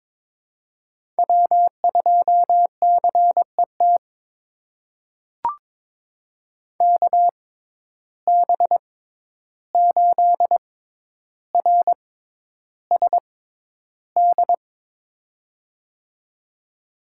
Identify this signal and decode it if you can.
Morse code, heard as a keyed tone, about 22 wpm, spaced as for 11 wpm, 700 Hz. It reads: W2CET KB8RSD
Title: USA Call Signs - 6x Character Spacing - 22wpm